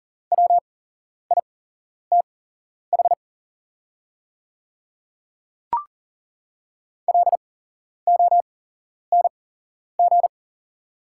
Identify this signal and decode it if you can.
Morse code, heard as a keyed tone, about 41 wpm, spaced as for 11 wpm, 690 Hz. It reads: WITH LONG
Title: 8x Character Spacing - Top 300 Words - 40wpm